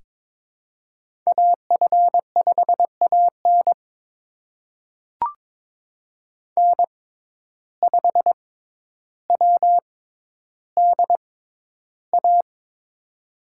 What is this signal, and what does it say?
Morse code, heard as a keyed tone, about 22 wpm, spaced as for 11 wpm, 700 Hz. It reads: AF5AN N5WDA
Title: USA Call Signs - 5 Characters - 6x Character Spacing - 22wpm